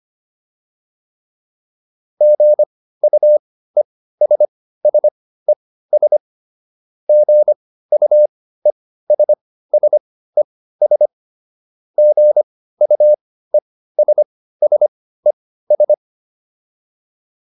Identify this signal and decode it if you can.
Morse code, heard as a keyed tone, about 25 wpm, spaced as for 15 wpm, 600 Hz. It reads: GUESSES GUESSES GUESSES